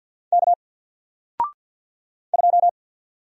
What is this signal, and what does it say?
Morse code, heard as a keyed tone, about 48 wpm, 700 Hz. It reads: K 2
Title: Single Character (Letter:Number:Punctuation:Prosign) 50wpm